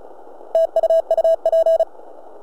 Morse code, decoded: TU UP